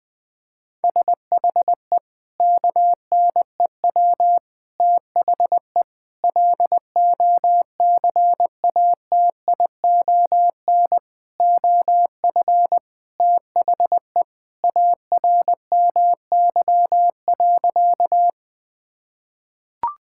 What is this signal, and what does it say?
Morse code, heard as a keyed tone, about 20 wpm, 700 Hz. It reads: SHE KNEW THE LOCATION OF THE ARMY.
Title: Sentences from Top 1000 Words - Review Part 6 - 20wpm